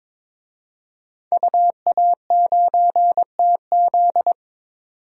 Morse code, heard as a keyed tone, about 22 wpm, 700 Hz. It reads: UA9TZ
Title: DXCC-100 Call Signs - 5 Characters - 22wpm